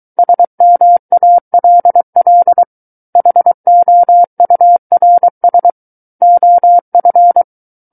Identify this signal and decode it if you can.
Morse code, heard as a keyed tone, about 23 wpm, 700 Hz. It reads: SMALL HOURS OF
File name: Book One / 14-Chapter 14 - 23 WPM